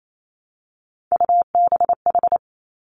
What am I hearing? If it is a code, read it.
Morse code, heard as a keyed tone, about 28 wpm, 700 Hz. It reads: UBH
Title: Three Challenging Letter-Number Combinations 28wpm